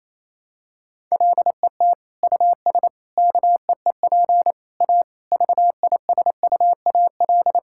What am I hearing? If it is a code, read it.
Morse code, heard as a keyed tone, about 28 wpm, 700 Hz. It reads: LET US KEEP A VISUAL
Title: Sentences from Top 1000 Words - Review Part 6 - 28wpm